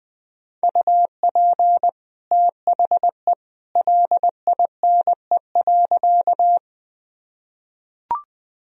Morse code, heard as a keyed tone, about 21 wpm, 700 Hz. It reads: UP THE LINE.
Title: Sentences from Top 1000 Words - Review Part 1 - 20wpm